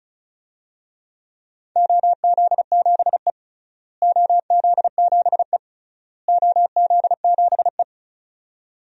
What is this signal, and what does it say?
Morse code, heard as a keyed tone, about 36 wpm, 700 Hz. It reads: OZ7E OZ7E OZ7E
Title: DXCC-100 Call Signs - 4 Characters - Repeated 3x - 35wpm